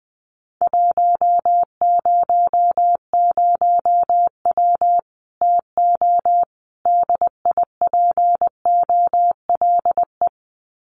Morse code, heard as a keyed tone, about 20 wpm, 700 Hz. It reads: 100W TO DIPOLE
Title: Common QSO Phrases 20wpm